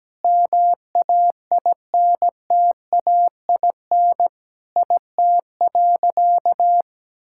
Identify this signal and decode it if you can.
Morse code, heard as a keyed tone, about 17 wpm, 700 Hz. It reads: MAINTAIN IT.
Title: Sentences from Top 1000 Words - Review Part 5 - 17wpm